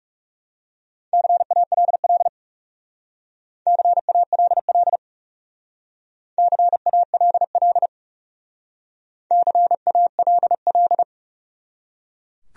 Morse code, heard as a keyed tone, about 37 wpm, 700 Hz. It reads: CALL CALL CALL CALL
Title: Speed-Racing Top 100 Words 30wpm